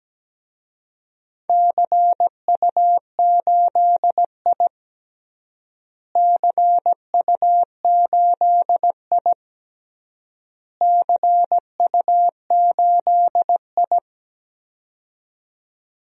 Morse code, heard as a keyed tone, about 17 wpm, 700 Hz. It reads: CU8I CU8I CU8I
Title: DXCC-100 Call Signs - 4 Characters - Repeated 3x - 17wpm